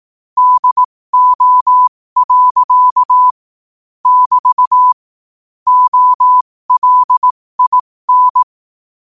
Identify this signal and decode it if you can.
Morse code, heard as a keyed tone, about 18 wpm, spaced as for 15 wpm, 1000 Hz. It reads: DO. <BT> OLIN
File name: qotdfiles / 15wpm